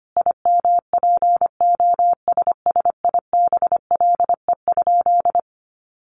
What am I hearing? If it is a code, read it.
Morse code, heard as a keyed tone, about 25 wpm, 700 Hz. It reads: IMPOSSIBLE?
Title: Sentences from Top 1100 Words 25wpm